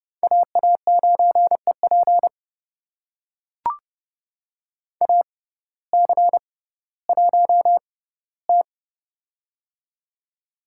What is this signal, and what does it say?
Morse code, heard as a keyed tone, about 30 wpm, spaced as for 17 wpm, 700 Hz. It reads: AA9EP AC1T
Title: USA Call Signs - 6x Character Spacing - 30wpm